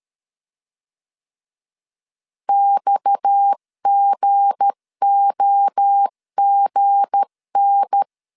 Morse code, heard as a keyed tone, about 13 wpm, 800 Hz. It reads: XGOGN
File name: LEZIONI / 7 GF9Z / 7 G 12wpm 2055